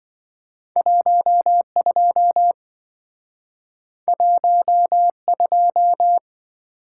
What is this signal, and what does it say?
Morse code, heard as a keyed tone, about 26 wpm, 700 Hz. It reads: 12 12